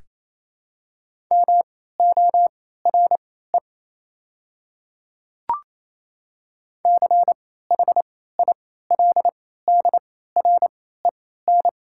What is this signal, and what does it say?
Morse code, heard as a keyed tone, about 28 wpm, spaced as for 16 wpm, 690 Hz. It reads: MORE CHILDREN
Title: 3x Character Spacing - Top 300 Words - 28wpm